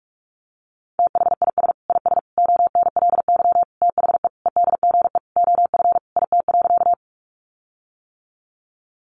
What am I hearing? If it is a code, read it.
Morse code, heard as a keyed tone, about 44 wpm, 700 Hz. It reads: THIS IS ONLY THE EDGE OF IT.
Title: Sentences from Top 1300 Words 45wpm